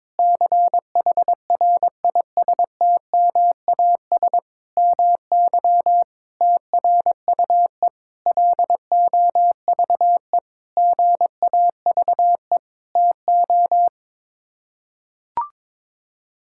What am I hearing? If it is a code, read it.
Morse code, heard as a keyed tone, about 22 wpm, 700 Hz. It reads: CHRISTMAS MY TRUE LOVE GAVE TO
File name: practice / 22wpm / Sets of 8 Words - 22wpm